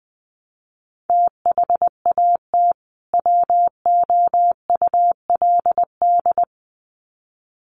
Morse code, heard as a keyed tone, about 20 wpm, 700 Hz. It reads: THAT WOULD
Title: Sets of 2 Words 20wpm